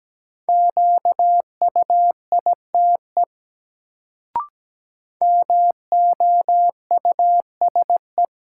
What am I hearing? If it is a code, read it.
Morse code, heard as a keyed tone, about 17 wpm, 700 Hz. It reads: QUITE MOUSE
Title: Sentences from Top 700 Words - New Word Drill - 17wpm